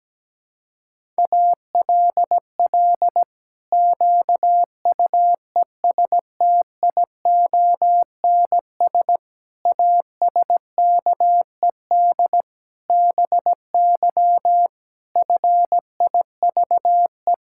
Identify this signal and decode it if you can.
Morse code, heard as a keyed tone, about 17 wpm, 700 Hz. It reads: ALL QUESTIONS ASKED BY FIVE
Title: Pangram Challenge v2 - 17wpm